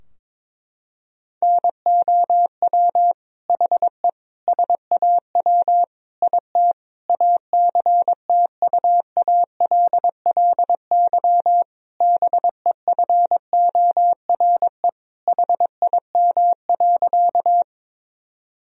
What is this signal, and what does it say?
Morse code, heard as a keyed tone, about 22 wpm, 700 Hz. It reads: NOW HE SAW IT ACTUALLY BEFORE HIM.
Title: Sentences from Top 1000 Words - Review Part 4 - 22wpm